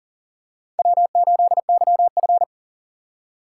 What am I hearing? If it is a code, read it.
Morse code, heard as a keyed tone, about 38 wpm, 700 Hz. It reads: W8YF
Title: USA Call Signs - 4 Characters - 8x Character Spacing - 40wpm